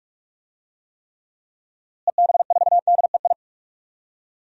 Morse code, heard as a keyed tone, about 46 wpm, 700 Hz. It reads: EB4DEI
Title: DXCC-100 Call Signs - 6 Characters - Repeated 3x - 45wpm